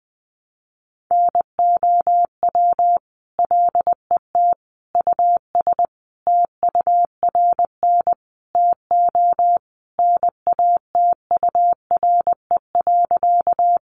Morse code, heard as a keyed tone, about 20 wpm, 700 Hz. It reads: NOW LET US TURN TO NATURE.
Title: Sentences from Top 600 Words 20wpm